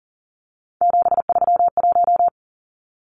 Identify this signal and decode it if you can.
Morse code, heard as a keyed tone, about 40 wpm, 700 Hz. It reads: 731
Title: Numbers - 3-Digit 40wpm